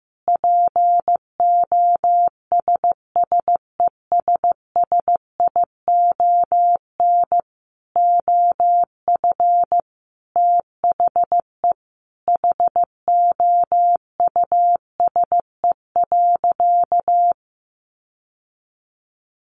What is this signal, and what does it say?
Morse code, heard as a keyed tone, about 15 wpm, 700 Hz. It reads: POSSESSION OF THE HOUSE.
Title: Sentences from Top 1400 Words 15wpm